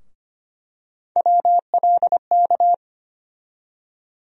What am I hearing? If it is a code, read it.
Morse code, heard as a keyed tone, about 25 wpm, 700 Hz. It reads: WLK
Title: USA Call Sign Postfixes - 3 Letters 25wpm